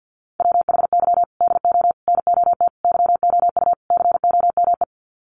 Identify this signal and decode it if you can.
Morse code, heard as a keyed tone, about 48 wpm, 700 Hz. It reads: WHY DO NOT YOU COME
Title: Sets of 5 Words Encore2 50wpm